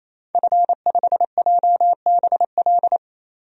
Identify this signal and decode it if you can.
Morse code, heard as a keyed tone, about 28 wpm, 700 Hz. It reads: F5JBL
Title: DXCC-100 Call Signs - 5 Characters - 28wpm